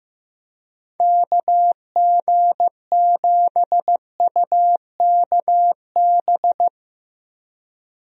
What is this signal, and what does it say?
Morse code, heard as a keyed tone, about 15 wpm, 700 Hz. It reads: KG7UKB
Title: USA Call Signs - 6 Characters - Repeated 3x - 15wpm